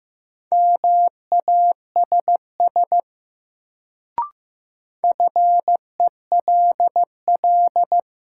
Morse code, heard as a keyed tone, about 15 wpm, 700 Hz. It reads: MASS FELL
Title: Sets of 5 Words Encore - New Word Drill - 15wpm